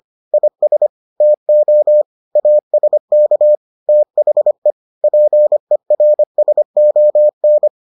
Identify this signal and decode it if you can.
Morse code, heard as a keyed tone, about 25 wpm, 600 Hz. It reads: IS TO ASK THE PERSON